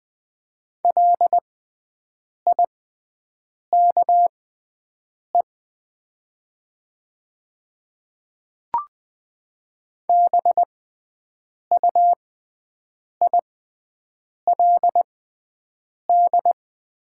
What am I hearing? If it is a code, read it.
Morse code, heard as a keyed tone, about 20 wpm, spaced as for 7 wpm, 700 Hz. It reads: LIKE BUILD
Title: 6x Character Spacing - Top 300 Words - 20wpm